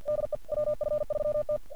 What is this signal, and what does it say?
Morse code, heard as a keyed tone, about 33 wpm, 600 Hz. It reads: DE WF3T
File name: archived / CRSnet